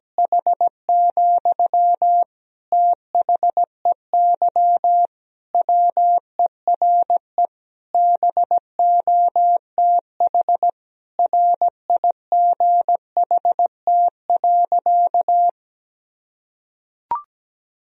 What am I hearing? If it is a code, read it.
Morse code, heard as a keyed tone, about 17 wpm, 700 Hz. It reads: H, THEY WERE BOTH RIGHT.
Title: Sentences from Top 1000 Words - Review Part 1 - 17wpm